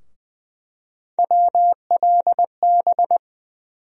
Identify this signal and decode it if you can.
Morse code, heard as a keyed tone, about 20 wpm, 700 Hz. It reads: WLB
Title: USA Call Sign Postfixes - 3 Letters 20wpm